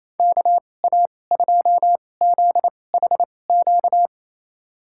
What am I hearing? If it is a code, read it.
Morse code, heard as a keyed tone, about 28 wpm, spaced as for 20 wpm, 690 Hz. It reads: KA2ZHQ
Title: USA Call Signs - 6 Characters - 2x Character Spacing - 28wpm